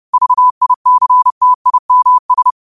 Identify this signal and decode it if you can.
Morse code, heard as a keyed tone, about 30 wpm, 1000 Hz. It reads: VICTIMS
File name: qotdfiles / 30wpm